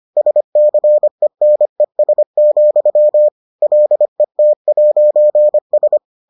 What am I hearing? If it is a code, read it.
Morse code, heard as a keyed tone, about 25 wpm, 600 Hz. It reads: SCENES, LET'S